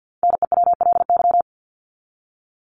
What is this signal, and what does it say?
Morse code, heard as a keyed tone, about 49 wpm, 700 Hz. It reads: NEWLY
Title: LY Suffix - Top 100 Words with Base Words 50wpm